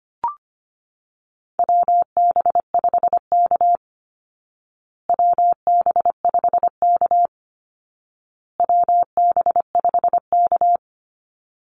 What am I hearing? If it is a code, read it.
Morse code, heard as a keyed tone, about 25 wpm, 700 Hz. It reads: WB5K WB5K WB5K
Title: Call Signs Repeated 2x - 25wpm